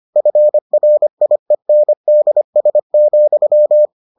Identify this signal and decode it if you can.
Morse code, heard as a keyed tone, about 25 wpm, 600 Hz. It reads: FRIENDS,